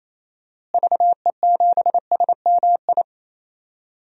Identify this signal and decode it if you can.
Morse code, heard as a keyed tone, about 28 wpm, 700 Hz. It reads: VE7SMI
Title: DXCC-100 Call Signs - 6 Characters - Repeated 2x - 28wpm